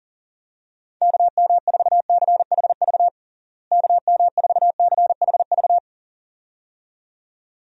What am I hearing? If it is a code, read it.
Morse code, heard as a keyed tone, about 40 wpm, 700 Hz. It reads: KM4CHV KM4CHV
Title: USA Call Signs - 6 Characters - Repeated 2x - 40wpm